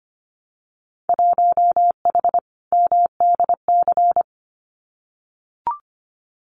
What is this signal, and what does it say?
Morse code, heard as a keyed tone, about 25 wpm, 700 Hz. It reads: 1H MDC
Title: Winter Field Day Exchanges 25wpm